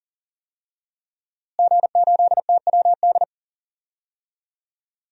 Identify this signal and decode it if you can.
Morse code, heard as a keyed tone, about 40 wpm, 700 Hz. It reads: G8TWD
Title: DXCC-100 Call Signs - 5 Characters - Repeated 2x - 40wpm